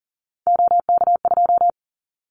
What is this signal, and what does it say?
Morse code, heard as a keyed tone, about 40 wpm, 700 Hz. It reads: OK2
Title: Callsign Prefixes for Top 25 DXCC - Repeated 3x - 40wpm